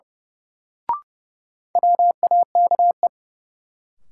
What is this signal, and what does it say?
Morse code, heard as a keyed tone, about 31 wpm, 700 Hz. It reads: WAKE
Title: Sets of 4 Words - New Word Drill - 30wpm